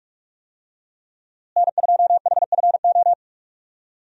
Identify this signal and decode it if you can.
Morse code, heard as a keyed tone, about 46 wpm, 700 Hz. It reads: N2HFO
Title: USA Call Signs - 5 Characters - Repeated 2x - 45wpm